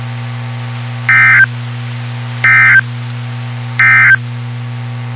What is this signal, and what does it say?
Emergency Alert System (SAME) tones, heard as harsh buzzing bursts. It is heard from 1.0 to 4.2 s.
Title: WXK91 NOAA Weather Radio